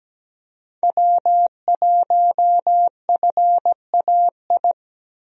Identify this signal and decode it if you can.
Morse code, heard as a keyed tone, about 17 wpm, 700 Hz. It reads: W1FAI
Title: USA Call Signs - 5 Characters - Repeated 2x - 17wpm